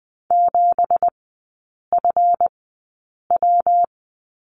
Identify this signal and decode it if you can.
Morse code, heard as a keyed tone, about 20 wpm, spaced as for 8 wpm, 690 Hz. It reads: 7FW
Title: 198 Lesson 32e - Letter k - Three Characters 20wpm